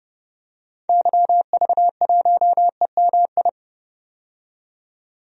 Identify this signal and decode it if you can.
Morse code, heard as a keyed tone, about 31 wpm, 700 Hz. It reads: YV1EMI
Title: DXCC-100 Call Signs - 6 Characters - 30wpm